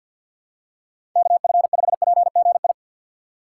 Morse code, heard as a keyed tone, about 48 wpm, 700 Hz. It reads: KF5PGI
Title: USA Call Signs - 6 Characters - 6x Character Spacing - 50wpm